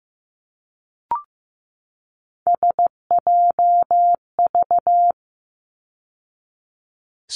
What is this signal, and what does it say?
Morse code, heard as a keyed tone, about 15 wpm, 700 Hz. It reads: SJV
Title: ARRL Sections 15wpm